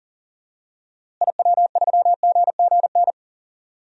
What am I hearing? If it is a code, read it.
Morse code, heard as a keyed tone, about 37 wpm, 700 Hz. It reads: IW3GGN
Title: DXCC-100 Call Signs - 6 Characters - Repeated 2x - 40wpm